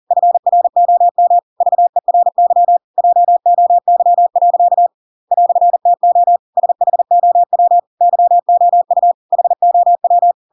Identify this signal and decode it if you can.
Morse code, heard as a keyed tone, about 40 wpm, 700 Hz. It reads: FROM VERY JOY. "TO SHOW YOU HOW
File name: Pinocchio / Pinocchio ch8 40 40 ik1hge